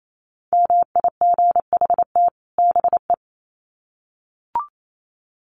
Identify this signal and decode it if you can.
Morse code, heard as a keyed tone, about 28 wpm, 700 Hz. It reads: MIGHT BE
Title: Sets of 2 Words 28wpm